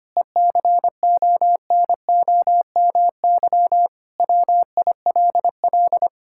Morse code, heard as a keyed tone, about 25 wpm, 700 Hz. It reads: ECONOMY WILL
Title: Sentences from Top 1000 Words - Review Part 5 - 25wpm